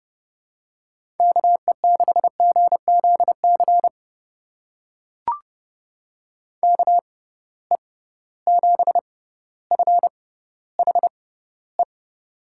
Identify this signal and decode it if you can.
Morse code, heard as a keyed tone, about 30 wpm, spaced as for 15 wpm, 700 Hz. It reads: KE6GZC KE7FHE
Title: USA Call Signs - 6 Characters - 6x Character Spacing - 30wpm